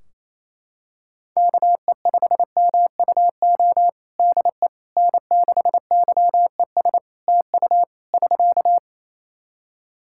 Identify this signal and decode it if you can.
Morse code, heard as a keyed tone, about 28 wpm, 700 Hz. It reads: KE5MUO DE N6YES TU <SK>
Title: K1USN SST QSO Practice 28wpm